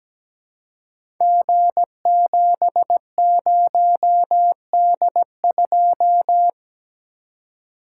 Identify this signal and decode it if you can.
Morse code, heard as a keyed tone, about 17 wpm, 700 Hz. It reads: G70D2
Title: Code Groups - 5 Letter:Numbers - 17wpm